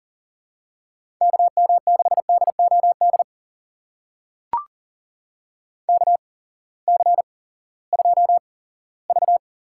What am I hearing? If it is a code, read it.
Morse code, heard as a keyed tone, about 40 wpm, spaced as for 20 wpm, 700 Hz. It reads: KM6DOD KC2V
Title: USA Call Signs - 6 Characters - 8x Character Spacing - 40wpm